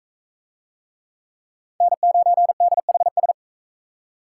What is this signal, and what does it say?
Morse code, heard as a keyed tone, about 39 wpm, 700 Hz. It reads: N9DHS